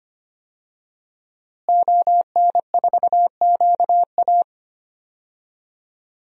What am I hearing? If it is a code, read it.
Morse code, heard as a keyed tone, about 25 wpm, 700 Hz. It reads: ON4QA
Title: DXCC-100 Call Signs - 5 Characters - Repeated 2x - 25wpm